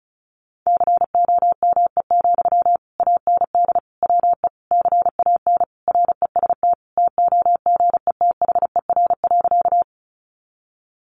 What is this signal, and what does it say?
Morse code, heard as a keyed tone, about 34 wpm, 700 Hz. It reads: COME, AND WE CAN REST TOGETHER.
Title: Sentences from Top 500 Words 35wpm